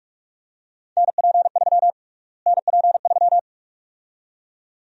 Morse code, heard as a keyed tone, about 45 wpm, 700 Hz. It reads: NP3 NP3
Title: USA Call Sign Prefixes - Repeated 2x - 45wpm